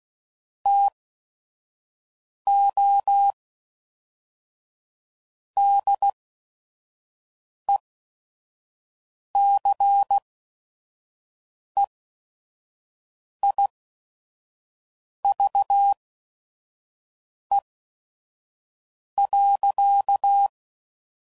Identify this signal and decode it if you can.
Morse code, heard as a keyed tone, about 16 wpm, spaced as for 5 wpm, 800 Hz. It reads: TO DECEIVE.